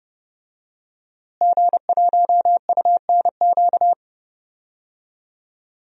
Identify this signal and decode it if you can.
Morse code, heard as a keyed tone, about 31 wpm, 700 Hz. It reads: G1UNQ
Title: DXCC-100 Call Signs - 5 Characters - Repeated 3x - 30wpm